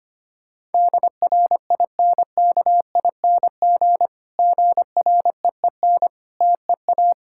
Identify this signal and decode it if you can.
Morse code, heard as a keyed tone, about 25 wpm, 700 Hz. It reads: DRINKING GREEN TEA
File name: practice / 25wpm / Difficult Sentences with Hints - 25wpm